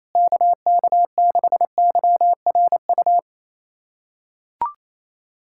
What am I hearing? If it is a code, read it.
Morse code, heard as a keyed tone, about 28 wpm, 700 Hz. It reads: KK6YRU
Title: USA Callsign DE Callsign - 6 Characters - 28wpm